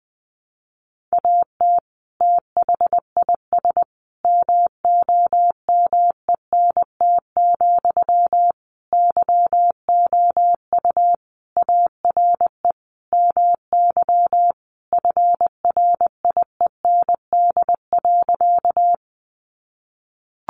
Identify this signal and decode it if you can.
Morse code, heard as a keyed tone, about 20 wpm, 700 Hz. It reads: AT THIS MOMENT, YOU ARE MY FRIEND.
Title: Sentences from Top 1000 Words 20wpm